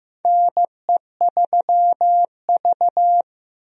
Morse code, heard as a keyed tone, about 15 wpm, 700 Hz. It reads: NE3V
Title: USA Call Signs - 4 Characters - 15wpm